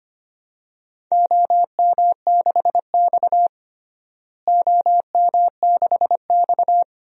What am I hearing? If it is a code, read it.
Morse code, heard as a keyed tone, about 25 wpm, 700 Hz. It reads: OM6X OM6X
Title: DXCC-100 Call Signs - 4 Characters - Repeated 2x - 25wpm